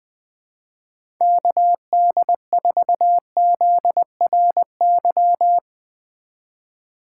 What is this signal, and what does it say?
Morse code, heard as a keyed tone, about 21 wpm, 700 Hz. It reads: KD4ZRY
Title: USA Call Signs - 2x Character Spacing - 20wpm